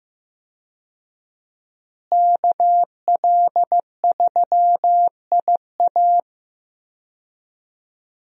Morse code, heard as a keyed tone, about 15 wpm, 700 Hz. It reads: KL3IA
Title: USA Call Signs - 5 Characters - Repeated 3x - 15wpm